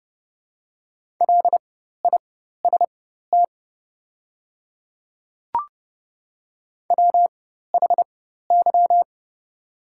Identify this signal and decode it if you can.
Morse code, heard as a keyed tone, about 30 wpm, spaced as for 14 wpm, 700 Hz. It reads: LIST WHY
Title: 4x Character Spacing - Top 500 Words - 30wpm